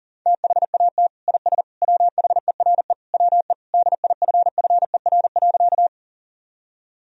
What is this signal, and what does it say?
Morse code, heard as a keyed tone, about 40 wpm, 700 Hz. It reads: THAT IS WHERE WE DIFFER.
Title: Sentences from Top 1000 Words - Review Part 1 - 40wpm